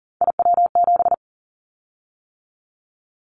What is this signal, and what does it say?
Morse code, heard as a keyed tone, about 40 wpm, 700 Hz. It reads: IW7
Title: Callsign Prefixes for Top 50 DXCC - 40wpm